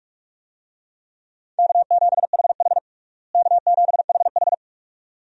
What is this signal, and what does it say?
Morse code, heard as a keyed tone, about 46 wpm, 700 Hz. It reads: K7HH K7HH
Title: USA Call Signs - 4 Characters - Repeated 2x - 45wpm